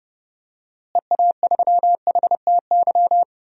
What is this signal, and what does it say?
Morse code, heard as a keyed tone, about 30 wpm, 700 Hz. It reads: EA3HTY
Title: DXCC-100 Call Signs - 6 Characters - Repeated 2x - 30wpm